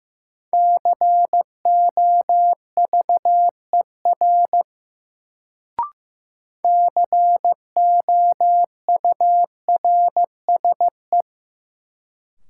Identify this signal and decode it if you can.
Morse code, heard as a keyed tone, about 15 wpm, 700 Hz. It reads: COVER COURSE
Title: Sentences from Top 500 Words - New Word Drill - 15wpm